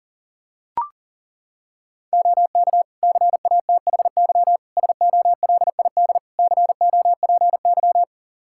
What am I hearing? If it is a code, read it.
Morse code, heard as a keyed tone, about 40 wpm, 700 Hz. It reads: OK CATHY SOLID COPY
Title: K1USN SST QSO Practice 40wpm